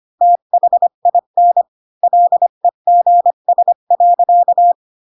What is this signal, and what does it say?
Morse code, heard as a keyed tone, about 25 wpm, 700 Hz. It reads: THIN LEGS.